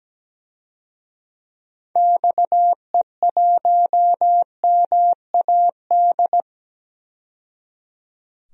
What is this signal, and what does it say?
Morse code, heard as a keyed tone, about 17 wpm, 700 Hz. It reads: XE1MAD
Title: DXCC-100 Call Signs - 6 Characters - 17wpm